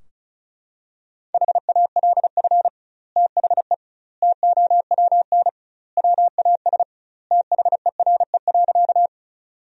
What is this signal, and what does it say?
Morse code, heard as a keyed tone, about 34 wpm, 700 Hz. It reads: HALF THE TOWN WAS THERE.
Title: 2x Word-Spacing - Sentences from Top 500 Words - 35wpm